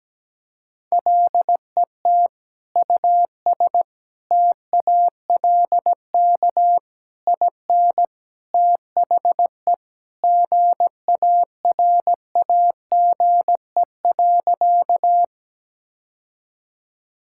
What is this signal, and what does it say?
Morse code, heard as a keyed tone, about 17 wpm, 700 Hz. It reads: LET US TALK IN THE GARAGE.
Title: Sentences from Top 1600 Words 17wpm